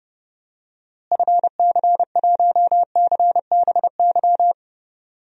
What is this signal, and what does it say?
Morse code, heard as a keyed tone, about 32 wpm, 700 Hz. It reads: FC1CBY
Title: DXCC-100 Call Signs - 6 Characters - Repeated 2x - 30wpm